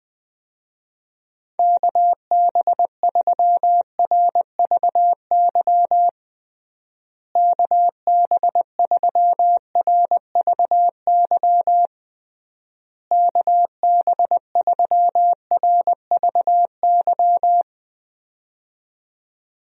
Morse code, heard as a keyed tone, about 20 wpm, 700 Hz. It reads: KB3RVY KB3RVY KB3RVY
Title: USA Call Signs - 6 Characters - Repeated 3x - 20wpm